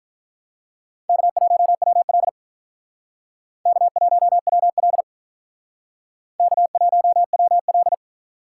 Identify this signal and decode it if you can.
Morse code, heard as a keyed tone, about 45 wpm, 700 Hz. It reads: K1WL K1WL K1WL